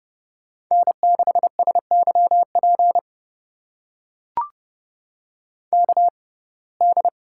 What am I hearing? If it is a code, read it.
Morse code, heard as a keyed tone, about 29 wpm, spaced as for 21 wpm, 700 Hz. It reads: N6SYP KD
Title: USA Call Signs - 5 Characters - 6x Character Spacing - 30wpm